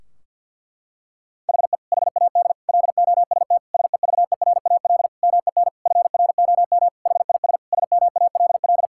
Morse code, heard as a keyed tone, about 50 wpm, 700 Hz. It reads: HE HAD LOST SEVERAL MEN FROM HIS SMALL
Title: Sentences from Top 1600 Words 50wpm